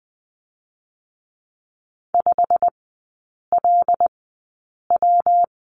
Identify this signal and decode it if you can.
Morse code, heard as a keyed tone, about 20 wpm, 700 Hz. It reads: 5 L W
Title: 121 Lesson 20e - Number 3 - Three Characters 20wpm